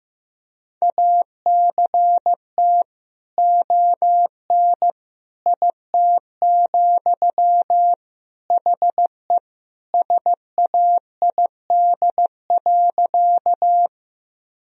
Morse code, heard as a keyed tone, about 15 wpm, 700 Hz. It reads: ACT ON IT, HE SAID.
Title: Sentences from Top 1000 Words - Review Part 1 - 15wpm